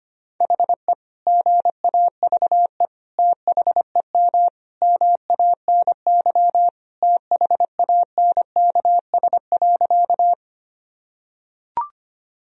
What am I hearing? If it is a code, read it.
Morse code, heard as a keyed tone, about 25 wpm, 700 Hz. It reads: HE GAVE THEM MANY THANKS.
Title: Sentences from Top 1000 Words - Review Part 3 - 25wpm